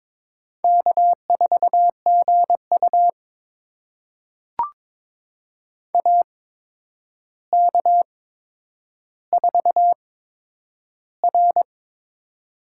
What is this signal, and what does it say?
Morse code, heard as a keyed tone, about 22 wpm, spaced as for 9 wpm, 700 Hz. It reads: K4GU AK4R
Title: USA Call Signs - 8x Character Spacing - 22wpm